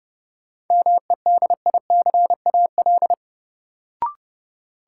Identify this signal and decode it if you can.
Morse code, heard as a keyed tone, about 32 wpm, 700 Hz. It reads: MEDICAL
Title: Sentences from Top 800 Words - New Word Drill - 30wpm